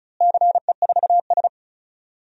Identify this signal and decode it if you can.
Morse code, heard as a keyed tone, about 34 wpm, 700 Hz. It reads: CE4S
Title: DXCC-100 Call Signs - 4 Characters - Repeated 2x - 35wpm